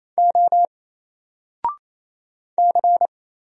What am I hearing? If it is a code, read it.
Morse code, heard as a keyed tone, about 27 wpm, 700 Hz. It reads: O C
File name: Single-Letter-Number-28wpm